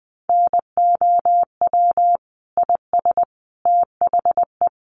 Morse code, heard as a keyed tone, about 20 wpm, 700 Hz. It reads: NOW IS THE
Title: Sets of 5 Words Encore2 20wpm